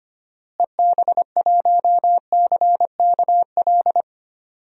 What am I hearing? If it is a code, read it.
Morse code, heard as a keyed tone, about 25 wpm, 700 Hz. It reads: EB1CKL
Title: DXCC-100 Call Signs - 6 Characters - Repeated 2x - 25wpm